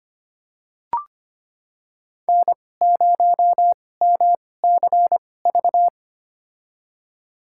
Morse code, heard as a keyed tone, about 25 wpm, spaced as for 18 wpm, 700 Hz. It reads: N0MCV
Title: USA Call Signs - 5 Characters - 2x Character Spacing - 25wpm